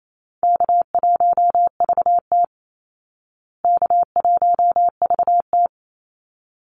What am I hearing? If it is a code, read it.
Morse code, heard as a keyed tone, about 28 wpm, 700 Hz. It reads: K1VT K1VT
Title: Easy Call Signs Repeated 2x - 28wpm